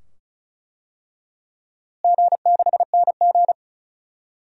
Morse code, heard as a keyed tone, about 34 wpm, 700 Hz. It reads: G6NG
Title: DXCC-100 Call Signs - 4 Characters - Repeated 2x - 35wpm